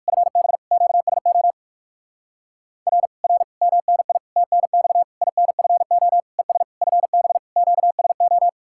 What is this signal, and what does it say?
Morse code, heard as a keyed tone, about 59 wpm, 700 Hz. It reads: FB QSO R R MNI TNX INFO ES FB QSO